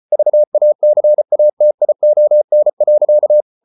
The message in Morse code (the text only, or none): VACATION.